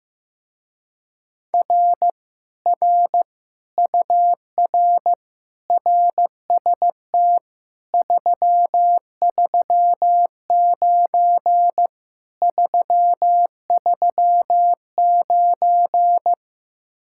Morse code, heard as a keyed tone, about 15 wpm, 700 Hz. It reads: R R UR RST 339 339
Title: Speed-Racing Common QSO Phrases Encore - 15wpm